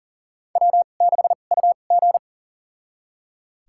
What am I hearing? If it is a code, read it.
Morse code, heard as a keyed tone, about 40 wpm, spaced as for 29 wpm, 700 Hz. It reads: W6UG
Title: USA Call Signs - 4 Characters - 2x Character Spacing - 40wpm